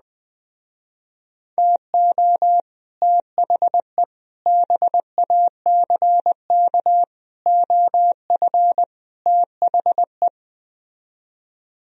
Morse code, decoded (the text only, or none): TO THE BACK OF THE